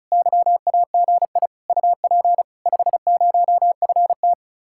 Morse code, heard as a keyed tone, about 43 wpm, 700 Hz. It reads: YAGI UP 50FT
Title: Speed-Racing Common QSO Phrases - 35wpm